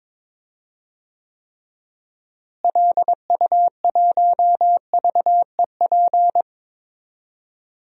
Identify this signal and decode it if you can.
Morse code, heard as a keyed tone, about 22 wpm, 700 Hz. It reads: LU1VEP
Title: DXCC-100 Call Signs - 6 Characters - Repeated 2x - 22wpm